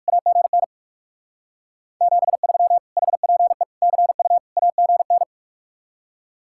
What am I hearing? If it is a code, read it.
Morse code, heard as a keyed tone, about 59 wpm, 700 Hz. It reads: AGN 73 HPE CU AGN